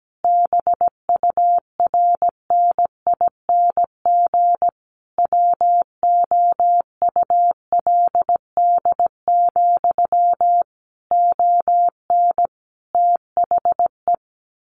Morse code, heard as a keyed tone, about 17 wpm, 700 Hz. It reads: BURNING WOULD, ON THE
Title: A Sound Of Thunder 17wpm